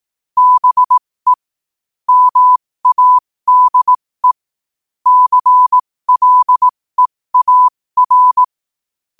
Morse code, heard as a keyed tone, about 18 wpm, spaced as for 15 wpm, 1000 Hz. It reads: BE MADE CLEAR
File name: qotdfiles / 15wpm